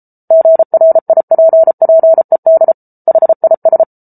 Morse code, heard as a keyed tone, about 34 wpm, 650 Hz. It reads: GRIPPED HIS